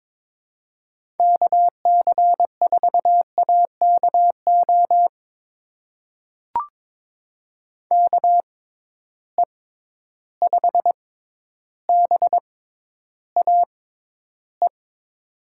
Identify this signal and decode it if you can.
Morse code, heard as a keyed tone, about 21 wpm, spaced as for 11 wpm, 700 Hz. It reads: KC4AKO KE5BAE
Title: USA Call Signs - 6 Characters - 6x Character Spacing - 22wpm